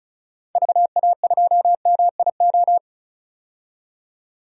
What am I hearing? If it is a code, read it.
Morse code, heard as a keyed tone, about 36 wpm, 700 Hz. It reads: VA2MIO
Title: DXCC-100 Call Signs - 6 Characters - 35wpm